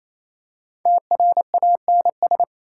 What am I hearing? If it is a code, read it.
Morse code, heard as a keyed tone, about 28 wpm, 700 Hz. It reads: TRANS
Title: Prefixes and Suffixes 28wpm